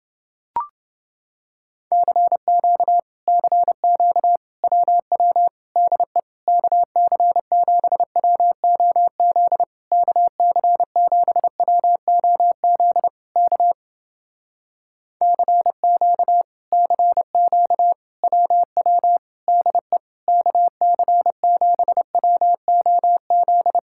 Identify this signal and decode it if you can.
Morse code, heard as a keyed tone, about 29 wpm, 700 Hz. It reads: CQ CQ WW DE KC7WOZ KC7WOZ K CQ CQ WW DE KC7WOZ